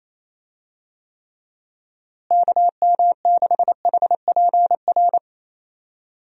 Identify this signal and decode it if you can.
Morse code, heard as a keyed tone, about 28 wpm, 700 Hz. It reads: KM6HPR
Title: USA Call Signs - 6 Characters - Repeated 3x - 28wpm